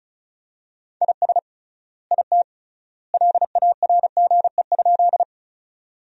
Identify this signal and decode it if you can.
Morse code, heard as a keyed tone, about 36 wpm, 700 Hz. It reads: IS IT LARGE?
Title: 3x Word-Spacing - Sentences from Top 200 Words - 35wpm